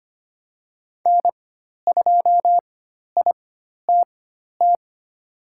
Morse code, heard as a keyed tone, about 25 wpm, spaced as for 12 wpm, 700 Hz. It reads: N2ITT
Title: USA Call Signs - 5 Characters - 4x Character Spacing - 25wpm